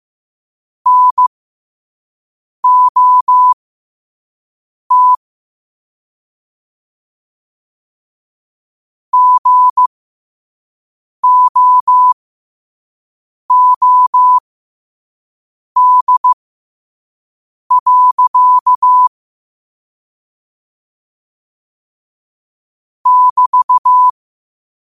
Morse code, heard as a keyed tone, about 15 wpm, spaced as for 5 wpm, 1000 Hz. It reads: NOT GOOD. <BT>